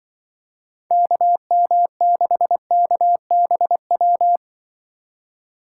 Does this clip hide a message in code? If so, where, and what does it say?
Yes: Morse code from 0.9 to 4.4 s, heard as a keyed tone, about 25 wpm, 700 Hz. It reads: KM6KBW